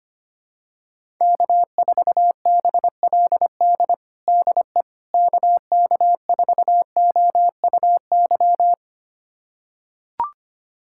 Morse code, heard as a keyed tone, about 25 wpm, 700 Hz. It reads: K4BLD DE KK4OUY
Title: USA Callsign DE Callsign 25wpm